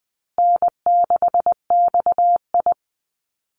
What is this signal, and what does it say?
Morse code, heard as a keyed tone, about 20 wpm, 700 Hz. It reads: N6XI
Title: Easy Call Signs Repeated 3x - 20wpm